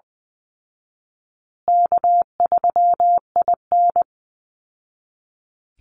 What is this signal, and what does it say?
Morse code, heard as a keyed tone, about 20 wpm, 700 Hz. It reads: K3IN
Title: Easy Call Signs Repeated 3x - 20wpm